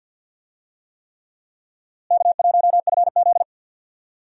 Morse code, heard as a keyed tone, about 52 wpm, 700 Hz. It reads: K1FZ
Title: USA Call Signs - 4 Characters - Repeated 3x - 50wpm